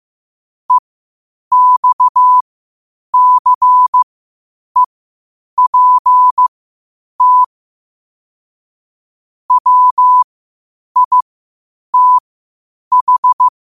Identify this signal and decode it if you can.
Morse code, heard as a keyed tone, about 15 wpm, spaced as for 8 wpm, 1000 Hz. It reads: EXCEPT WITH
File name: qotdfiles / 8wpm